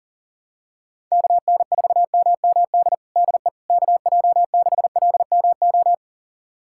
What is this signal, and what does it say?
Morse code, heard as a keyed tone, about 40 wpm, 700 Hz. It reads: KN4MMD DE KJ6LMO
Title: USA Callsign DE Callsign - 6 Characters - 40wpm